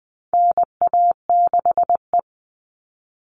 Morse code, heard as a keyed tone, about 20 wpm, 700 Hz. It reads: NA6E
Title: All Cut Numbers - 4-Digit 20wpm